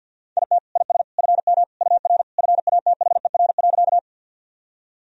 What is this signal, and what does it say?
Morse code, heard as a keyed tone, about 52 wpm, 700 Hz. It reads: IT IS FM UR FATHER.
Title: Sentences from Top 300 Words and Abbreviations 50wpm